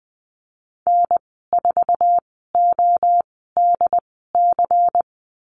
Morse code, heard as a keyed tone, about 20 wpm, spaced as for 15 wpm, 700 Hz. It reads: N4ODC
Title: 179 Lesson 29g - Slash - Call Signs 20wpm